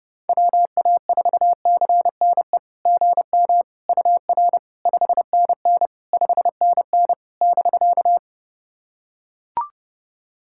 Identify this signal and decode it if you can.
Morse code, heard as a keyed tone, about 32 wpm, 700 Hz. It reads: WA4CNE GM UR 5NN 5NN <BK>
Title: SOTA QSO Practice 30wpm